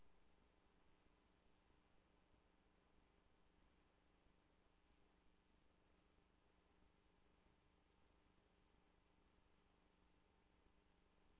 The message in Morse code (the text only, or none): none